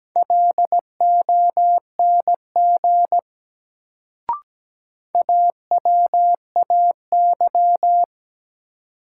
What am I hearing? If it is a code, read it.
Morse code, heard as a keyed tone, about 17 wpm, 700 Hz. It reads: LONG AWAY
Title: Sets of 2 Words - New Word Drill - 17wpm